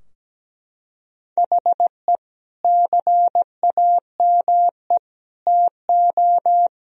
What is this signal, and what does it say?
Morse code, heard as a keyed tone, about 17 wpm, 700 Hz. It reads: HE CAME TO
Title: Sentences from Top 1000 Words - Review Part 3 - 17wpm